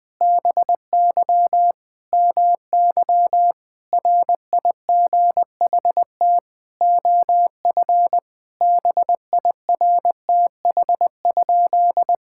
Morse code, heard as a keyed tone, about 20 wpm, 700 Hz. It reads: BY MY RIGHT OF BIRTH?
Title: Sentences from Top 1000 Words - Review Part 6 - 20wpm